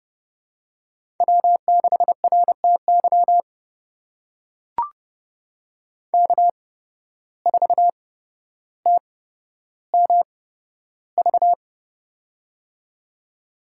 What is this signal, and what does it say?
Morse code, heard as a keyed tone, about 32 wpm, spaced as for 13 wpm, 700 Hz. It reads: W6RTY K4TMV
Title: USA Call Signs - 5 Characters - 8x Character Spacing - 30wpm